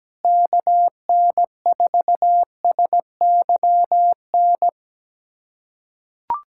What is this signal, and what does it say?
Morse code, heard as a keyed tone, about 17 wpm, 700 Hz. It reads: KN4SYN
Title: USA Callsign DE Callsign - 6 Characters - 17wpm